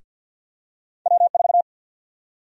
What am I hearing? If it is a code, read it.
Morse code, heard as a keyed tone, about 50 wpm, 700 Hz. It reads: W4
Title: USA Call Sign Prefixes - Repeated 2x - 50wpm